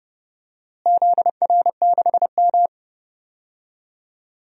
Morse code, heard as a keyed tone, about 30 wpm, 700 Hz. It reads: ZR6M
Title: DXCC-100 Call Signs - 4 Characters - Repeated 2x - 30wpm